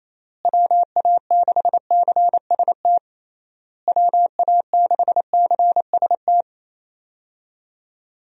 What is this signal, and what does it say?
Morse code, heard as a keyed tone, about 28 wpm, 700 Hz. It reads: WA6CST WA6CST
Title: USA Call Signs - 6 Characters - Repeated 3x - 28wpm